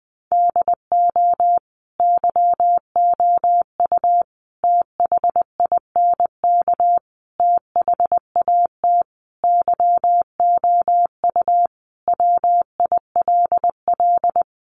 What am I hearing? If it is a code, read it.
Morse code, heard as a keyed tone, about 20 wpm, 700 Hz. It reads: DO YOU THINK THAT YOU WILL
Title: Sentences from Top 1500 Words 20wpm